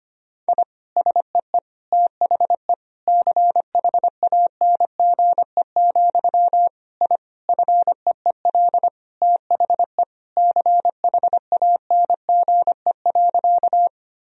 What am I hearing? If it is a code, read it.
Morse code, heard as a keyed tone, about 25 wpm, 700 Hz. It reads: I SEE THE CHANGE, I FEEL THE CHANGE.
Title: Sentences from Top 1000 Words - Review Part 1 - 25wpm